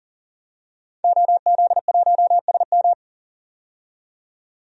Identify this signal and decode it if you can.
Morse code, heard as a keyed tone, about 40 wpm, 700 Hz. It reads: OZ1SM
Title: DXCC-100 Call Signs - 5 Characters - 40wpm